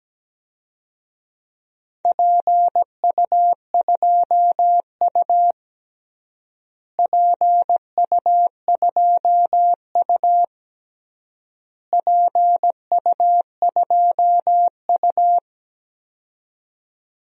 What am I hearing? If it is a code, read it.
Morse code, heard as a keyed tone, about 17 wpm, 700 Hz. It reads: PU2U PU2U PU2U
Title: DXCC-100 Call Signs - 4 Characters - Repeated 3x - 17wpm